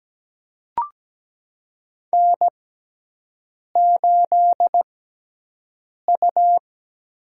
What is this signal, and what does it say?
Morse code, heard as a keyed tone, about 17 wpm, spaced as for 6 wpm, 700 Hz. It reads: N8U
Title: USA Call Signs - 6x Character Spacing - 17wpm